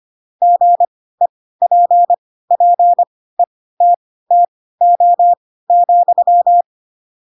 Morse code, heard as a keyed tone, about 25 wpm, spaced as for 16 wpm, 700 Hz. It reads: GEPPETTO,
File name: Pinocchio / Pinocchio ch7 10 25 ik1hge